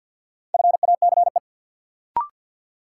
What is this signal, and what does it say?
Morse code, heard as a keyed tone, about 53 wpm, 700 Hz. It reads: FACE
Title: Sets of 3 Words - New Word Drill - 50wpm